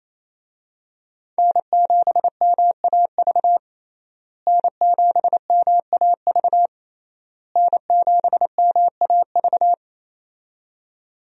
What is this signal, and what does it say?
Morse code, heard as a keyed tone, about 27 wpm, 700 Hz. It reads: N7MAV N7MAV N7MAV
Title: USA Call Signs - 5 Characters - Repeated 3x - 28wpm